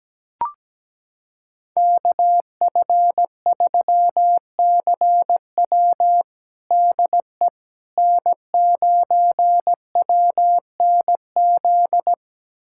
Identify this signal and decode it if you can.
Morse code, heard as a keyed tone, about 17 wpm, 700 Hz. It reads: KF3CW DE N9WNZ
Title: USA Callsign DE Callsign - 5 Characters - 17wpm